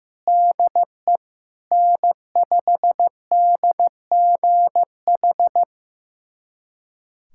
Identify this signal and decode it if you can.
Morse code, heard as a keyed tone, about 16 wpm, 700 Hz. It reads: DE N5DGH
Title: USA Callsign DE Callsign - 5 Characters - 15wpm